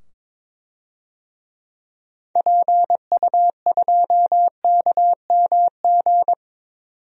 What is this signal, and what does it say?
Morse code, heard as a keyed tone, about 22 wpm, 700 Hz. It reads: PU2KMG
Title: DXCC-100 Call Signs - 6 Characters - 22wpm